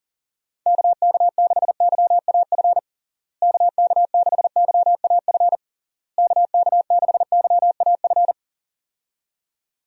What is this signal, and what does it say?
Morse code, heard as a keyed tone, about 40 wpm, 700 Hz. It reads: KK6YAF KK6YAF KK6YAF
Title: USA Call Signs - 6 Characters - Repeated 3x - 40wpm